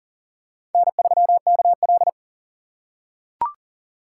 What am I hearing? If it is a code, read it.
Morse code, heard as a keyed tone, about 39 wpm, 700 Hz. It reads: N3KL
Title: USA Call Signs - 4 Characters - 4x Character Spacing - 40wpm